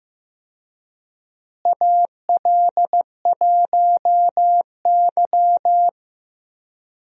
Morse code, heard as a keyed tone, about 16 wpm, 700 Hz. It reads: AL1Y
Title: USA Call Signs - 4 Characters - Repeated 3x - 15wpm